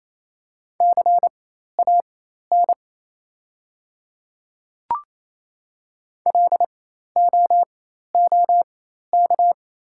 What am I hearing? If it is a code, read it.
Morse code, heard as a keyed tone, about 28 wpm, spaced as for 13 wpm, 700 Hz. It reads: CAN LOOK
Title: 4x Character Spacing - Top 100 Words - 28wpm